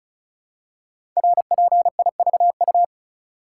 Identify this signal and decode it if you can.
Morse code, heard as a keyed tone, about 36 wpm, 690 Hz. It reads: RPIVU
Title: Code Groups - 5 Letters - 35wpm